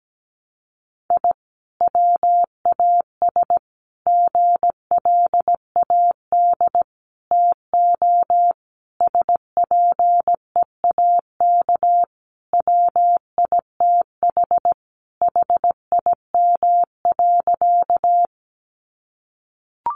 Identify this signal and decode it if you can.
Morse code, heard as a keyed tone, about 17 wpm, 700 Hz. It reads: I WAS GLAD TO SPEAK WITH HIM.
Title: Sentences from Top 1100 Words 17wpm